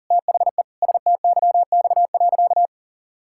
Morse code, heard as a keyed tone, about 40 wpm, 700 Hz. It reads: THE STYX.
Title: Pangram Challenge v2 - 40wpm